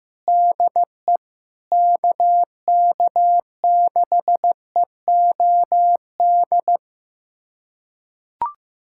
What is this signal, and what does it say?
Morse code, heard as a keyed tone, about 15 wpm, 700 Hz. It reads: DE KK6EOD
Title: USA Callsign DE Callsign 15wpm